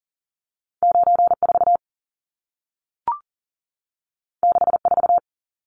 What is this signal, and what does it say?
Morse code, heard as a keyed tone, about 40 wpm, 700 Hz. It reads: 94 64
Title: Numbers - 2-Digit 40wpm